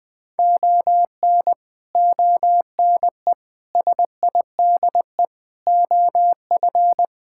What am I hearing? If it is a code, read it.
Morse code, heard as a keyed tone, about 21 wpm, 700 Hz. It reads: ON ONE SIDE OF
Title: Build A Phrase - 5-Word Phrases - 20wpm